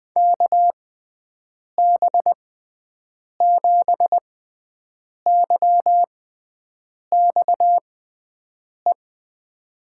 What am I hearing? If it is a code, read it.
Morse code, heard as a keyed tone, about 20 wpm, spaced as for 7 wpm, 700 Hz. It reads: KB7YXE
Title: USA Call Signs - 6x Character Spacing - 20wpm